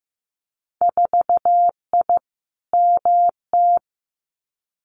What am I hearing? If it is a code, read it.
Morse code, heard as a keyed tone, about 15 wpm, 700 Hz. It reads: 4I MT
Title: Winter Field Day Exchanges Encore 15wpm